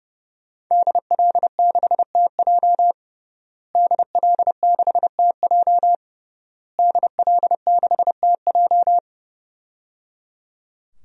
Morse code, heard as a keyed tone, about 30 wpm, 700 Hz. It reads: DL6TJ DL6TJ DL6TJ
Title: DXCC-100 Call Signs - 5 Characters - Repeated 3x - 30wpm